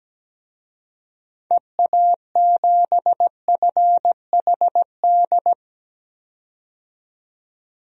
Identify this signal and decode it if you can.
Morse code, heard as a keyed tone, about 17 wpm, 700 Hz. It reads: EA7FHD
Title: DXCC-100 Call Signs - 6 Characters - 17wpm